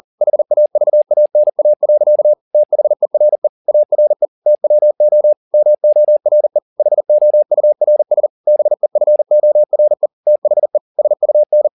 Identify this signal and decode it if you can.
Morse code, heard as a keyed tone, about 40 wpm, 600 Hz. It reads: HAVANA. THERE ARE TWO MORE HOURS BEFORE THE SUN